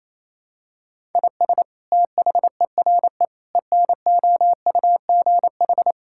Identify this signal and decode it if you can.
Morse code, heard as a keyed tone, about 28 wpm, 700 Hz. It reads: IS THERE ENOUGH
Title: Sentences from Top 1000 Words - Review Part 5 - 28wpm